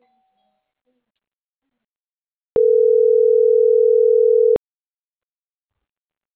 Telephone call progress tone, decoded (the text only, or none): ringback tone